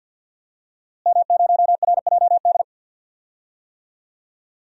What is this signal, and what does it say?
Morse code, heard as a keyed tone, about 53 wpm, 700 Hz. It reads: M0RJD
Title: DXCC-100 Call Signs - 5 Characters - Repeated 3x - 50wpm